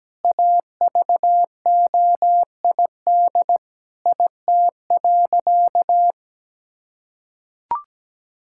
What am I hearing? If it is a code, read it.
Morse code, heard as a keyed tone, about 17 wpm, 700 Hz. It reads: AVOID IT.
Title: Sentences from Top 1000 Words - Review Part 6 - 17wpm